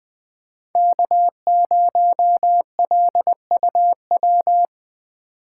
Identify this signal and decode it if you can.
Morse code, heard as a keyed tone, about 20 wpm, 700 Hz. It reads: K0LUW
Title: K1USN SST QSO Practice - 20wpm